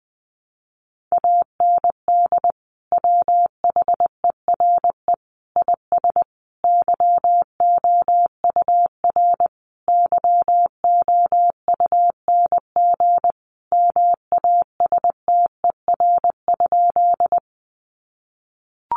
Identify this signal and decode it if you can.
Morse code, heard as a keyed tone, about 20 wpm, 700 Hz. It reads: AND WHERE IS YOUR YOUNG MASTER?
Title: Sentences from Top 1300 Words 20wpm